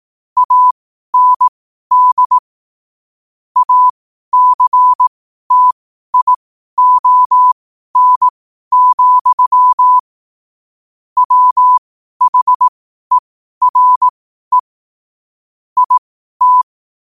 Morse code, heard as a keyed tone, about 18 wpm, spaced as for 12 wpm, 1000 Hz. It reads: AND ACTION, WHERE IT
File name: qotdfiles / 12wpm